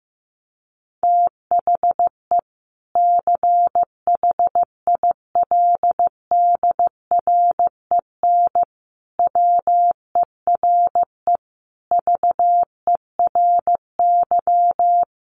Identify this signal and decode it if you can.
Morse code, heard as a keyed tone, about 15 wpm, 700 Hz. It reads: THE CHILDREN WERE VERY
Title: Sentences from Top 1300 Words 15wpm